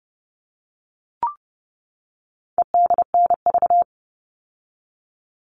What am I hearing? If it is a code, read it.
Morse code, heard as a keyed tone, about 30 wpm, 700 Hz. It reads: EDNV
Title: All Cut Numbers - 4-Digit 30wpm